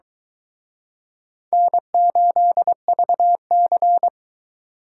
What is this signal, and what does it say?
Morse code, heard as a keyed tone, about 23 wpm, 700 Hz. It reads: N8VC